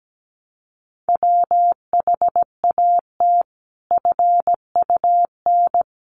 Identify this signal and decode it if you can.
Morse code, heard as a keyed tone, about 17 wpm, 700 Hz. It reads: WHAT FUN
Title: Sentences from Top 700 Words 17wpm